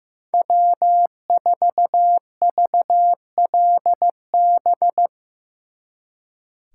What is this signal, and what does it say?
Morse code, heard as a keyed tone, about 17 wpm, 700 Hz. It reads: W4VLB
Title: Speed Racing - USA 1x3 Call Signs 15wpm